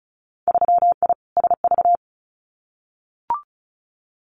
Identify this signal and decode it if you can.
Morse code, heard as a keyed tone, about 36 wpm, 700 Hz. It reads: 3I SV
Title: Winter Field Day Exchanges Encore 35wpm